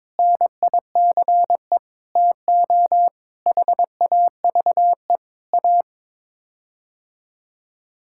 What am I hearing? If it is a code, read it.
Morse code, heard as a keyed tone, about 22 wpm, 700 Hz. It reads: NICE TO HAVE A
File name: practice / 22wpm / Sets of 7 Words - 22wpm